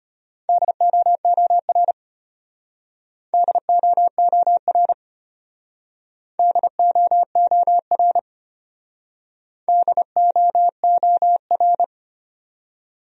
Morse code, heard as a keyed tone, about 33 wpm, 700 Hz. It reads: DOOR DOOR DOOR DOOR